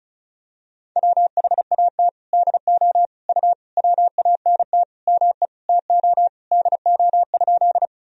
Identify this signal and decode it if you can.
Morse code, heard as a keyed tone, about 36 wpm, 700 Hz. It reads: WHAT DO U WANT ME TO DO?
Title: Sentences from Top 200 Words and Abbreviations 35wpm